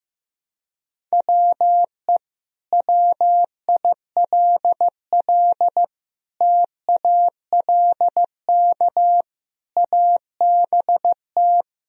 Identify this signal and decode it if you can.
Morse code, heard as a keyed tone, about 15 wpm, 700 Hz. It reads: WE WILL TALK ABT
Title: Sentences from Top 400 Words and Abbreviations 15wpm